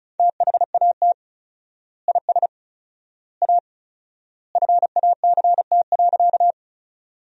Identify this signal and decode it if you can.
Morse code, heard as a keyed tone, about 34 wpm, 700 Hz. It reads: THAT IS A FACT.
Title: 4x Word-Spacing - Sentences from Top 500 Words - 35wpm